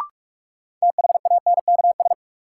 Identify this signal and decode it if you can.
Morse code, heard as a keyed tone, about 41 wpm, 700 Hz. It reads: THANKS
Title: Sets of 5 Words - New Word Drill - 45wpm